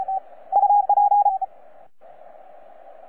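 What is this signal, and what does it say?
Morse code, heard as a keyed tone, about 33 wpm, 820 Hz. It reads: UP
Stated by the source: XT2TT DX station on 12 meters CW, from DX World.net.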